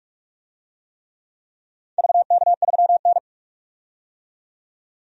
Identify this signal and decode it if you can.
Morse code, heard as a keyed tone, about 46 wpm, 700 Hz. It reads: VK3N
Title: DXCC-100 Call Signs - 4 Characters - 45wpm